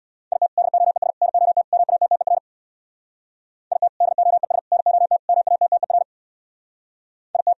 Morse code, heard as a keyed tone, about 64 wpm, 700 Hz. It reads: IT DOES NOT MATTER IT DOES NOT MATTER IT